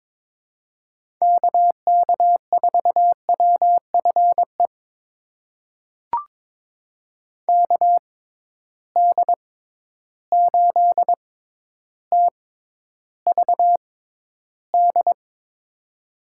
Morse code, heard as a keyed tone, about 22 wpm, spaced as for 11 wpm, 700 Hz. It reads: KK4WFE KD8TVD
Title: USA Call Signs - 6 Characters - 6x Character Spacing - 22wpm